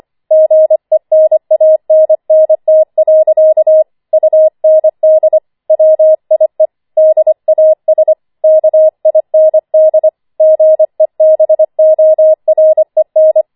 Morse code, heard as a keyed tone, about 25 wpm, 630 Hz. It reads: GENANNT. UND WIE DAS KIND GEBOREN
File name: 25wpm